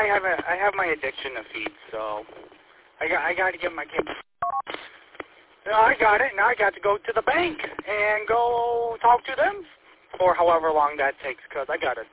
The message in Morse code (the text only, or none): none